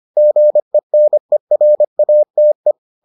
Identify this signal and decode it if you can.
Morse code, heard as a keyed tone, about 25 wpm, 600 Hz. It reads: GENERATE